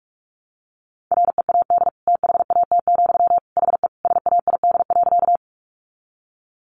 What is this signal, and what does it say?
Morse code, heard as a keyed tone, about 44 wpm, 700 Hz. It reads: READ THAT, HE SAID.
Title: Sentences from Top 200 Words 45wpm